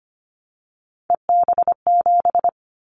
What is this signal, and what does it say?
Morse code, heard as a keyed tone, about 25 wpm, 700 Hz. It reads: EB7
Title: Callsign Prefixes for Top 25 DXCC - 25wpm